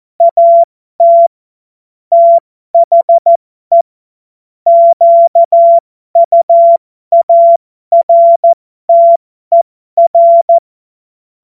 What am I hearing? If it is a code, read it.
Morse code, heard as a keyed tone, about 14 wpm, spaced as for 12 wpm, 680 Hz. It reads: AT THE QUARTER